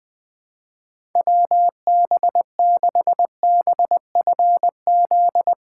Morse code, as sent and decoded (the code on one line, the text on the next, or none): .-- -... -.... -... ..-. --..
WB6BFZ